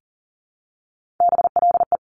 Morse code, heard as a keyed tone, about 42 wpm, 700 Hz. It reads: BLE
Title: Top Three Letter Combinations in Words 40wpm